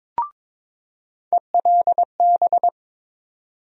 Morse code, heard as a keyed tone, about 22 wpm, 700 Hz. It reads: ELB
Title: USA Call Sign Postfixes - 3 Letters 22wpm